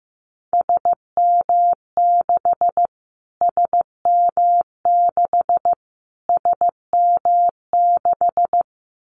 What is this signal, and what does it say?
Morse code, heard as a keyed tone, about 15 wpm, 700 Hz. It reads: SM6 SM6 SM6
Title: Callsign Prefixes for Top 25 DXCC - Repeated 3x - 15wpm